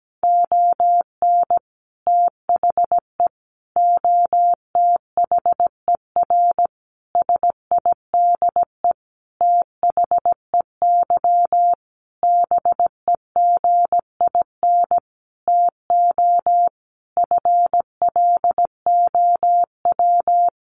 Morse code, heard as a keyed tone, about 17 wpm, 700 Hz. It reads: ON THE OTHER SIDE THEY BEGIN TO FLOW
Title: Sentences from Top 1200 Words 17wpm